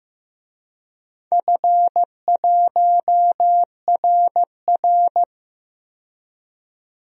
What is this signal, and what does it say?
Morse code, heard as a keyed tone, about 15 wpm, 700 Hz. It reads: F1RR
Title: DXCC-100 Call Signs - 4 Characters - 15wpm